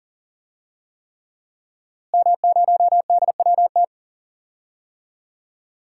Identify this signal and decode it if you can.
Morse code, heard as a keyed tone, about 40 wpm, 700 Hz. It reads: M0DWT
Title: DXCC-100 Call Signs - 5 Characters - Repeated 3x - 40wpm